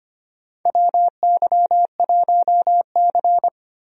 Morse code, heard as a keyed tone, about 25 wpm, 700 Hz. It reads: WY1C
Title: USA Call Signs - 4 Characters - Repeated 3x - 25wpm